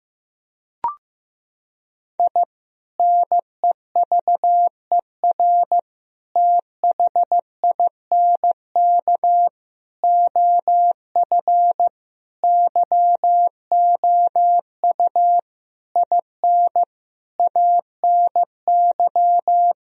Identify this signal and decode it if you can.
Morse code, heard as a keyed tone, about 16 wpm, 700 Hz. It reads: I NEVER THINK OF YOU IN ANY
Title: Sentences from Top 1000 Words - Review Part 1 - 15wpm